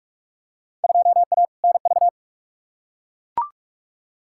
Morse code, heard as a keyed tone, about 41 wpm, 700 Hz. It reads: 2A NV
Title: Field Day Exchanges - 45wpm